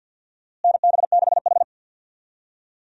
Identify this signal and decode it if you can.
Morse code, heard as a keyed tone, about 52 wpm, 700 Hz. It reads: NB6H
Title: USA Call Signs - 4 Characters - Repeated 3x - 50wpm